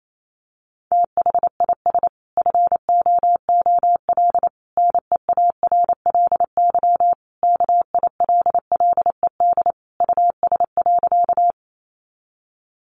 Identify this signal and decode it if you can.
Morse code, heard as a keyed tone, about 28 wpm, 700 Hz. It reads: THIS FOOL NEARLY KILLED US.
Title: A Sound Of Thunder 28wpm